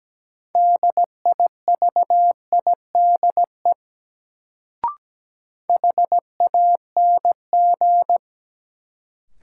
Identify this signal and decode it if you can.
Morse code, heard as a keyed tone, about 17 wpm, 700 Hz. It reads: DIVIDE HANG
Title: Sentences from Top 1600 Words - New Word Drill - 17wpm